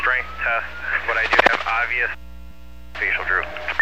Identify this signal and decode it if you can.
Emergency Alert System (SAME) tones, heard as harsh buzzing bursts, with speech. It is heard from 0.7 to 3.4 s.